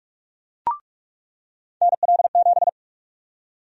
Morse code, heard as a keyed tone, about 44 wpm, 700 Hz. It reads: NL7
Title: USA Call Sign Prefixes - 45wpm